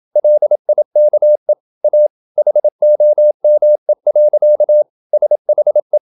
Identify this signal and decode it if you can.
Morse code, heard as a keyed tone, about 28 wpm, 600 Hz. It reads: LIKE A HOME. SHE
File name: Never 0027wpm